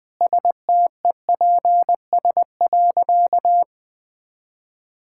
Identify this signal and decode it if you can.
Morse code, heard as a keyed tone, about 21 wpm, 700 Hz. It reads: STEPS.
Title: Easy Sentences with Hints - 20wpm